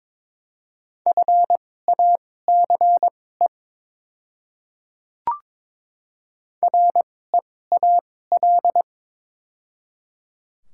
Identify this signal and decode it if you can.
Morse code, heard as a keyed tone, about 21 wpm, spaced as for 16 wpm, 700 Hz. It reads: FACE REAL
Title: 2x Character Spacing - Top 300 Words - 22wpm